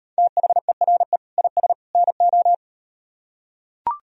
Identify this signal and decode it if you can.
Morse code, heard as a keyed tone, about 39 wpm, 700 Hz. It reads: THERE IS NO